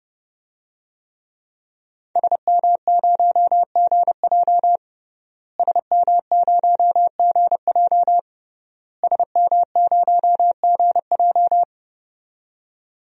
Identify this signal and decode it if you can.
Morse code, heard as a keyed tone, about 30 wpm, 700 Hz. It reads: SM0GJ SM0GJ SM0GJ
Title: DXCC-100 Call Signs - 5 Characters - Repeated 3x - 30wpm